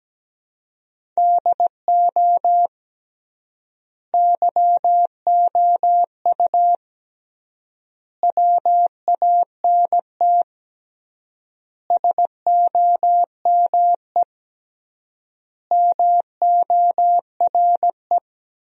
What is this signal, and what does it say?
Morse code, heard as a keyed tone, about 17 wpm, 700 Hz. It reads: DO YOU WANT SOME MORE
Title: 3x Word-Spacing - Sentences from Top 300 Words - 17wpm